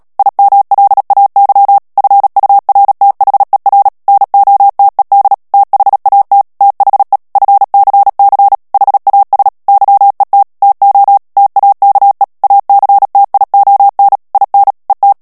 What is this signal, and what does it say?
Morse code, heard as a keyed tone, about 37 wpm, 800 Hz. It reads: IMLAY FURTHER NOTED THAT THE FCC HAS YET TO TAKE ACTION IN ET
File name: morse mp3 / ANGLAIS texte clair 3 vitesse 35 mots minute